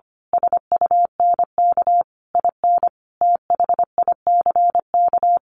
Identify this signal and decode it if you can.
Morse code, heard as a keyed tone, about 25 wpm, 700 Hz. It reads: SUNK IN THICK
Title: A Sound Of Thunder 25wpm